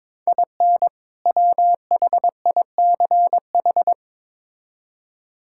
Morse code, heard as a keyed tone, about 22 wpm, 700 Hz. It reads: IN WHICH
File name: Build A Phrase - 2-Word Phrases - 22wpm